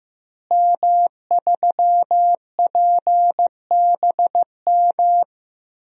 Morse code, heard as a keyed tone, about 15 wpm, 700 Hz. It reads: M3PBM
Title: DXCC-100 Call Signs - 5 Characters - Repeated 2x - 15wpm